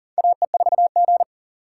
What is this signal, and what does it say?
Morse code, heard as a keyed tone, about 40 wpm, 700 Hz. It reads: AE4G
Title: USA Call Signs - 4 Characters - 6x Character Spacing - 40wpm